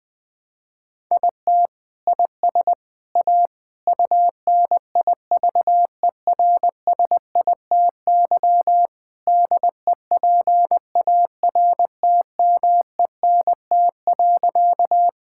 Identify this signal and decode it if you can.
Morse code, heard as a keyed tone, about 20 wpm, 700 Hz. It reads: IT IS A UNIVERSITY DEPARTMENT.
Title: Sentences from Top 1000 Words - Review Part 5 - 20wpm